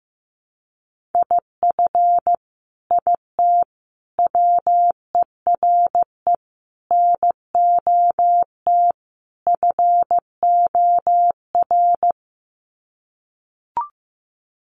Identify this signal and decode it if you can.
Morse code, heard as a keyed tone, about 15 wpm, 700 Hz. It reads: IF IT WERE NOT FOR
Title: Sets of 5 Words 15wpm